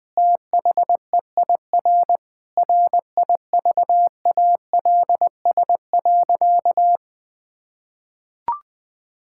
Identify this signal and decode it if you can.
Morse code, heard as a keyed tone, about 20 wpm, 700 Hz. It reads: THEIR RIVALS.
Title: Sentences with 7-Word Phrases - 20wpm